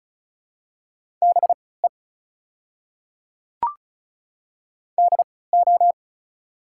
Morse code, heard as a keyed tone, about 36 wpm, spaced as for 20 wpm, 700 Hz. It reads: BE DO
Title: 3x Character Spacing - Top 200 Words - 35wpm